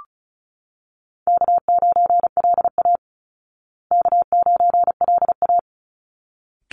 Morse code, heard as a keyed tone, about 34 wpm, 700 Hz. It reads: K9LA K9LA
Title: Easy Call Signs Repeated 2x - 35wpm